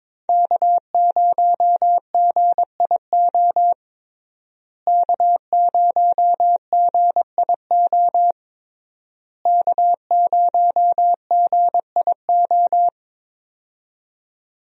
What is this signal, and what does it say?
Morse code, heard as a keyed tone, about 22 wpm, 700 Hz. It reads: K0GIO K0GIO K0GIO
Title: USA Call Signs - 5 Characters - Repeated 3x - 22wpm